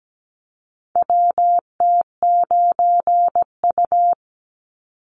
Morse code, heard as a keyed tone, about 17 wpm, 700 Hz. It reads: WT9U
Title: Call Signs - 17wpm